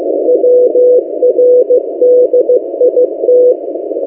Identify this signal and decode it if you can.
Morse code, heard as a keyed tone, about 15 wpm, 500 Hz. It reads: WRDIT
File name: M08a- -1355utc-8096khz